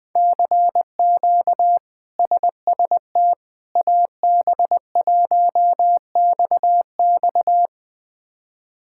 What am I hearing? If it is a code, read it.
Morse code, heard as a keyed tone, about 20 wpm, 700 Hz. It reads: CQ SST AB1XX
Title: K1USN SST QSO Practice - 20wpm